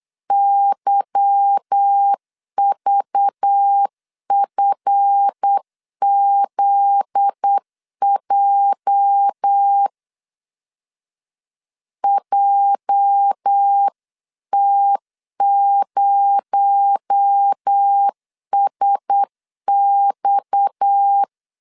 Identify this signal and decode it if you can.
Morse code, heard as a keyed tone, about 8 wpm, 800 Hz. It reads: YVFZJ JT0SX
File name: CW / DET / 1 9 1500C S C 8wpm 5653